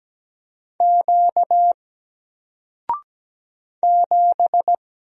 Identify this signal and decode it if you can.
Morse code, heard as a keyed tone, about 17 wpm, 700 Hz. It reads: Q 7
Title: Single Letter-Number 17wpm